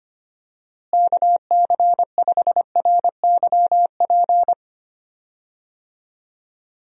Morse code, heard as a keyed tone, about 25 wpm, 700 Hz. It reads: KC5RYP
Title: USA Call Signs - 6 Characters - Repeated 2x - 25wpm